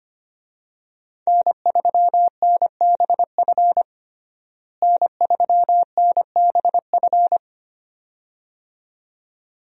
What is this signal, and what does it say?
Morse code, heard as a keyed tone, about 25 wpm, 700 Hz. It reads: N3NBF N3NBF
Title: USA Call Signs - 5 Characters - Repeated 2x - 25wpm